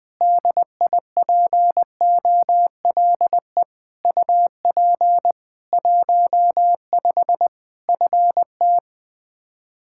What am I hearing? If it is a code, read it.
Morse code, heard as a keyed tone, about 20 wpm, 700 Hz. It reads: DIPOLE UP 15 FT
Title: Practice QSO - Exchange 2 - Info - 20wpm